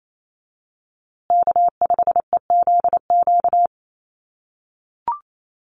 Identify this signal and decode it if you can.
Morse code, heard as a keyed tone, about 28 wpm, 700 Hz. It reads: K5EZQ
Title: USA Call Signs Repeated 2x 28wpm